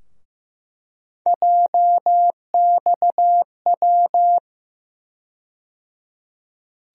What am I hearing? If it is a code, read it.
Morse code, heard as a keyed tone, about 15 wpm, 700 Hz. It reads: JXW
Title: USA Call Sign Postfixes - 3 Letters 15wpm